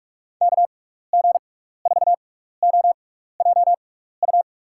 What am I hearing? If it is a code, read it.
Morse code, heard as a keyed tone, about 46 wpm, spaced as for 16 wpm, 700 Hz. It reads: KG4OJU
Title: USA Call Signs - 6x Character Spacing - 45wpm